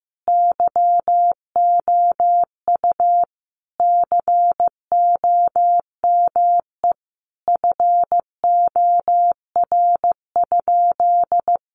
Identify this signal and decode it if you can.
Morse code, heard as a keyed tone, about 14 wpm, 700 Hz. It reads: YOU COME FOR?
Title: Sentences from Top 100 Words 15wpm